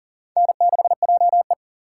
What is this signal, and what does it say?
Morse code, heard as a keyed tone, about 40 wpm, 700 Hz. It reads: N6JE
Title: USA Call Signs - 4 Characters - 6x Character Spacing - 40wpm